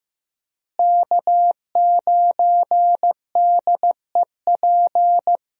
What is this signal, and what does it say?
Morse code, heard as a keyed tone, about 15 wpm, 700 Hz. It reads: K9DEP
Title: USA Call Signs - 5 Characters - Repeated 3x - 15wpm